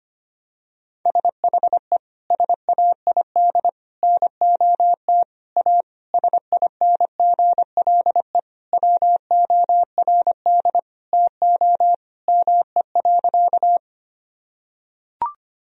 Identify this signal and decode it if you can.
Morse code, heard as a keyed tone, about 25 wpm, 700 Hz. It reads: SHE SAID NOT A SINGLE WORD TO ME.
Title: Sentences from Top 1000 Words - Review Part 4 - 25wpm